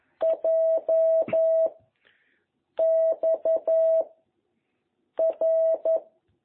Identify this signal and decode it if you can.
Morse code, heard as a keyed tone, about 11 wpm, 660 Hz. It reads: J X R